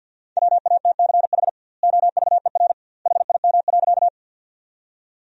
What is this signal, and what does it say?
Morse code, heard as a keyed tone, about 50 wpm, 700 Hz. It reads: WATCH OVER HIM.
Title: 2x Word-Spacing - Sentences from Top 300 Words - 50wpm